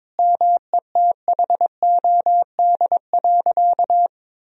Morse code, heard as a keyed tone, about 22 wpm, 700 Hz. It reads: METHOD.
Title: Sentences from Top 1000 Words - Review Part 4 - 22wpm